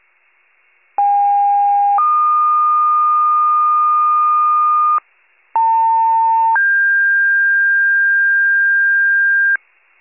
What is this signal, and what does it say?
Two-tone page, heard as a one-second tone, then a three-second tone, about 810 then 1220 Hz, about 900 then 1600 Hz, with noise. It is heard from 1.0 to 9.6 s.